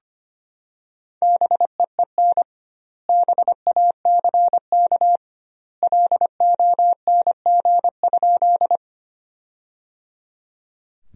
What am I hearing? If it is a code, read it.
Morse code, heard as a keyed tone, about 25 wpm, 700 Hz. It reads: BEEN BACK LONG?
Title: 2x Word-Spacing - Sentences from Top 200 Words - 25wpm